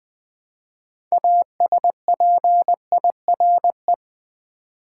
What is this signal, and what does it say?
Morse code, heard as a keyed tone, about 20 wpm, 700 Hz. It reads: ASPIRE
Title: 6 Letter Words - First 3 Letters Given - 20wpm